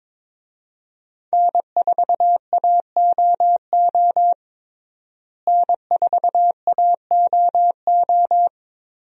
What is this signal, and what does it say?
Morse code, heard as a keyed tone, about 22 wpm, 700 Hz. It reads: N4AOO N4AOO
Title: USA Call Signs - 5 Characters - Repeated 2x - 22wpm